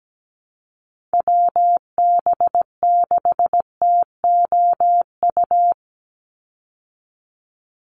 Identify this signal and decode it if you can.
Morse code, heard as a keyed tone, about 17 wpm, 700 Hz. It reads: WB6TOU
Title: Call Signs Repeated 2x - 17wpm